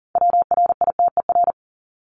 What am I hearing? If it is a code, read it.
Morse code, heard as a keyed tone, about 40 wpm, 700 Hz. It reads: WRITER
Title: Top 1751-2000 Words 40wpm